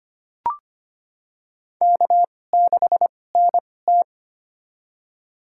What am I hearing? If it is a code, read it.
Morse code, heard as a keyed tone, about 25 wpm, spaced as for 18 wpm, 700 Hz. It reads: K6NT
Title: USA Call Signs - 4 Characters - 2x Character Spacing - 25wpm